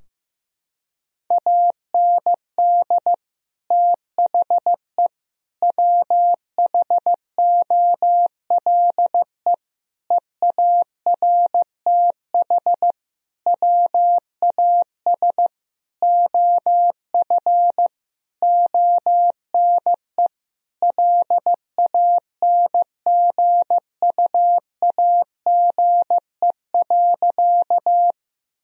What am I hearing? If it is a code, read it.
Morse code, heard as a keyed tone, about 15 wpm, 700 Hz. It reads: AND THE WHOLE EARTH WAS OF ONE LANGUAGE.
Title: Sentences from Top 1000 Words - Review Part 3 - 15wpm